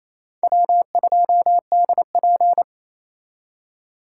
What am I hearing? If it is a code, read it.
Morse code, heard as a keyed tone, about 36 wpm, 700 Hz. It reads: W2DP